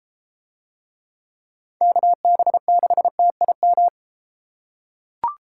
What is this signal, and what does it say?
Morse code, heard as a keyed tone, about 33 wpm, 700 Hz. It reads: KB6TIM